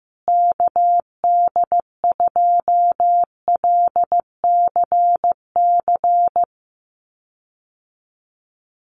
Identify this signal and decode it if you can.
Morse code, heard as a keyed tone, about 15 wpm, 700 Hz. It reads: KD2LCC
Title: USA Call Signs Repeated 2x 15wpm